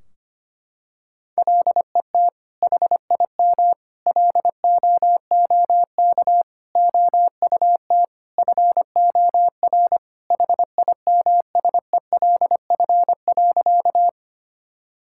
Morse code, heard as a keyed tone, about 25 wpm, 700 Hz. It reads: LET HIM LOOK OUT FOR HIMSELF.
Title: Sentences from Top 1600 Words 25wpm